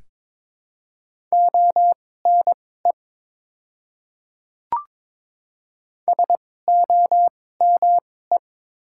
Morse code, heard as a keyed tone, about 22 wpm, spaced as for 16 wpm, 700 Hz. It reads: ONE SOME
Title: 2x Character Spacing - Top 100 Words - 22wpm